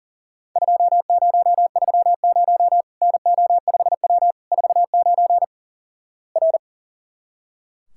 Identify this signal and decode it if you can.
Morse code, heard as a keyed tone, about 37 wpm, 700 Hz. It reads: 2030 NO5W 49 R
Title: Worked All Europe DX Contest - QTC Exchange - 40wpm